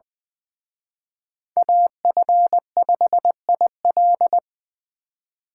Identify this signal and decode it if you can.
Morse code, heard as a keyed tone, about 24 wpm, 700 Hz. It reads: AF5IL
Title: Speed Racing - USA 2x2 Call Signs 20wpm